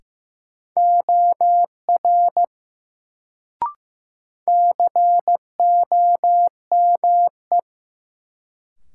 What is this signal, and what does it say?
Morse code, heard as a keyed tone, about 15 wpm, 700 Hz. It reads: OR COME
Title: Sentences from Top 100 Words - New Word Drill - 15wpm